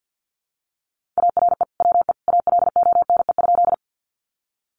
Morse code, heard as a keyed tone, about 50 wpm, 700 Hz. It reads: ARE WE ALONE?
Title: Sentences from Top 1000 Words 50wpm